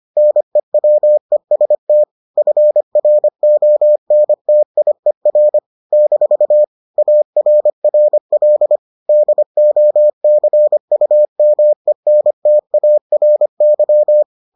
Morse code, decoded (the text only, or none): NEWEST FRONTIER - ARRL DOCUMENTARY